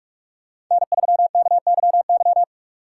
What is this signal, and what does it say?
Morse code, heard as a keyed tone, about 46 wpm, 700 Hz. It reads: N3KYY
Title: USA Call Signs - 4x Character Spacing - 45wpm